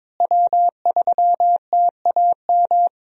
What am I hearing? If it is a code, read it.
Morse code, heard as a keyed tone, about 22 wpm, 700 Hz. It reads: W3TAM
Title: USA Call Signs - 5 Characters - Repeated 3x - 22wpm